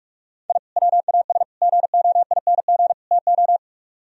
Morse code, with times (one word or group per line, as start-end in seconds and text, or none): 0.5-0.6 I
0.8-1.4 WAS
1.6-2.9 GOING
3.1-3.6 TO